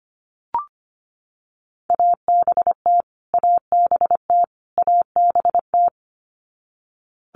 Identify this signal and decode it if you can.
Morse code, heard as a keyed tone, about 25 wpm, 700 Hz. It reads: ABT ABT ABT
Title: Abbreviations Repeated 3x - 25wpm